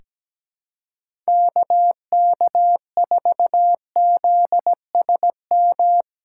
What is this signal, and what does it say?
Morse code, heard as a keyed tone, about 17 wpm, 700 Hz. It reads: KK4ZSM
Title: USA Callsign DE Callsign - 6 Characters - 17wpm